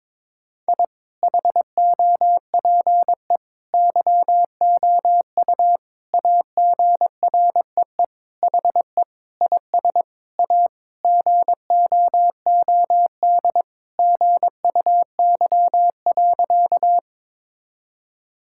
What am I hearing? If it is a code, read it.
Morse code, heard as a keyed tone, about 22 wpm, 700 Hz. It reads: I HOPE YOU AGREE HE IS A GOOD GUY.
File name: practice / 22wpm / Sentences with 8-Word Phrases - 22wpm